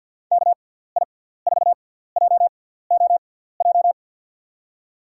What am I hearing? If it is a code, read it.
Morse code, heard as a keyed tone, about 49 wpm, spaced as for 17 wpm, 700 Hz. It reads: KI4JOJ
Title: USA Call Signs - 6 Characters - 6x Character Spacing - 50wpm